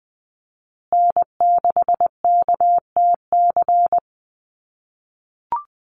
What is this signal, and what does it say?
Morse code, heard as a keyed tone, about 20 wpm, 700 Hz. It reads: N6KTC
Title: USA Call Signs Repeated 3x 20wpm